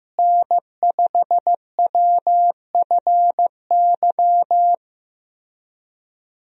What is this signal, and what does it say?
Morse code, heard as a keyed tone, about 15 wpm, 700 Hz. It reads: N5WFY
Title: USA Call Signs - 5 Characters - Repeated 3x - 15wpm